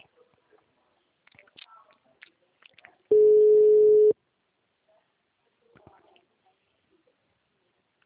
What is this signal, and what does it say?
Ringback tone (Europe, 425 Hz), with speech.
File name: out- -2021- - - .39281